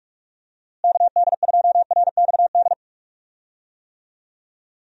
Morse code, heard as a keyed tone, about 45 wpm, 700 Hz. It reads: KD2RXD
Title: USA Call Signs - 6 Characters - Repeated 2x - 45wpm